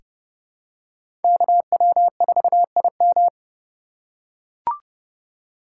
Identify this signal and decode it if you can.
Morse code, heard as a keyed tone, about 30 wpm, 700 Hz. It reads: KW4IM
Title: USA Call Signs - 3x Character Spacing - 30wpm